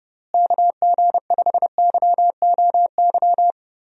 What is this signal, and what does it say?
Morse code, heard as a keyed tone, about 31 wpm, 700 Hz. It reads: KG5YOY
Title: USA Call Signs - 6 Characters - Repeated 3x - 30wpm